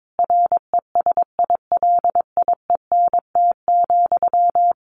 Morse code, heard as a keyed tone, about 22 wpm, 700 Hz. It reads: RESILIENT,
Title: A Sound Of Thunder 22wpm